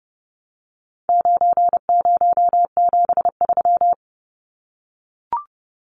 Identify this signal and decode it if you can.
Morse code, heard as a keyed tone, about 30 wpm, 700 Hz. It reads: 9073
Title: Numbers - 4-Digit 30wpm